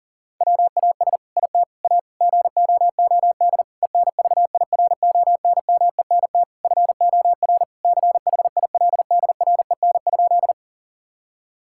Morse code, heard as a keyed tone, about 40 wpm, 700 Hz. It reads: WAS IT A GOOD ENVIRONMENT FOR CHILDREN?
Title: Sentences from Top 1000 Words - Review Part 5 - 40wpm